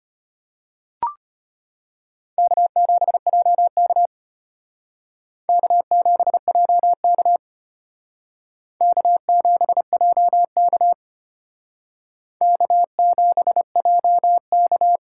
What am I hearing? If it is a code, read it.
Morse code, heard as a keyed tone, about 32 wpm, 700 Hz. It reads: K7JK K7JK K7JK K7JK